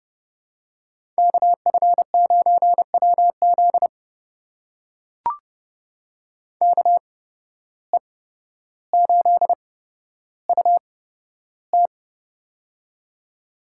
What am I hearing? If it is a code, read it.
Morse code, heard as a keyed tone, about 30 wpm, spaced as for 13 wpm, 700 Hz. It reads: KF9WZ KE8UT
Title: USA Call Signs - 5 Characters - 8x Character Spacing - 30wpm